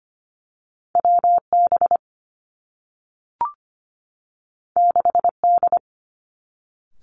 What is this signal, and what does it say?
Morse code, heard as a keyed tone, about 25 wpm, 700 Hz. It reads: W B 6 D
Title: Two Challenging Letter-Number Combinations 25wpm